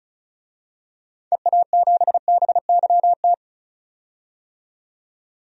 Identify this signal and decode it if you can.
Morse code, heard as a keyed tone, about 36 wpm, 700 Hz. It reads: EA7BYT
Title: DXCC-100 Call Signs - 6 Characters - 35wpm